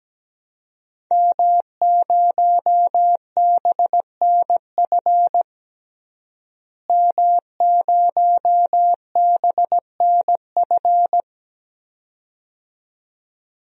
Morse code, heard as a keyed tone, about 17 wpm, 700 Hz. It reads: M0BNF M0BNF
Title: DXCC-100 Call Signs - 5 Characters - Repeated 2x - 17wpm